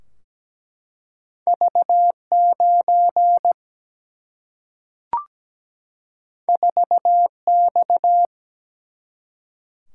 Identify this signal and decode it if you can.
Morse code, heard as a keyed tone, about 17 wpm, 700 Hz. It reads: V 9 4 X
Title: Single-Letter Single-Number 17wpm